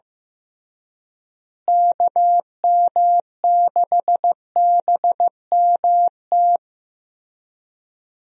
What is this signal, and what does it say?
Morse code, heard as a keyed tone, about 15 wpm, 700 Hz. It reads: KM6BMT
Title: USA Call Signs - 6 Characters - Repeated 3x - 15wpm